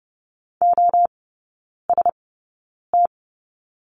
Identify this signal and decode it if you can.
Morse code, heard as a keyed tone, about 30 wpm, spaced as for 9 wpm, 700 Hz. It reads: OST
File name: Lesson 16e - Letter c - Three Characters 30wpm